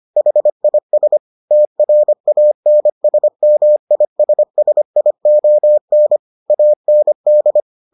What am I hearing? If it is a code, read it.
Morse code, heard as a keyed tone, about 25 wpm, 600 Hz. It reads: HIS TRANSMISSION AND